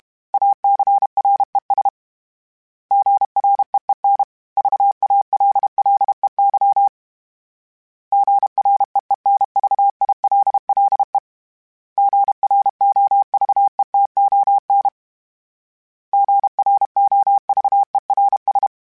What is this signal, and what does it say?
Morse code, heard as a keyed tone, about 32 wpm, 800 Hz. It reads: ACRES GREEN VALLEY GREENVILLE GROVETON GROVERS
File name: fromSMs / FromSMs mp3s / Cities@30